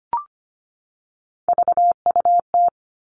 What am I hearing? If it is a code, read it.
Morse code, heard as a keyed tone, about 25 wpm, 700 Hz. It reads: VUT
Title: All Cut Numbers - 3-Digit 25wpm